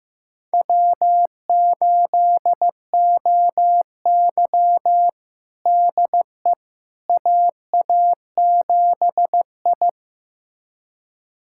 Morse code, heard as a keyed tone, about 15 wpm, 700 Hz. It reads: W8OY DE AA7I
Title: USA Callsign DE Callsign - 4 Characters - 15wpm